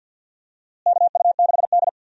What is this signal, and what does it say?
Morse code, heard as a keyed tone, about 50 wpm, 700 Hz. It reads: KU6D
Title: USA Call Signs - 4 Characters - Repeated 3x - 50wpm